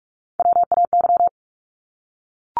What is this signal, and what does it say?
Morse code, heard as a keyed tone, about 44 wpm, 700 Hz. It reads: WAY
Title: Top 500 Words 45wpm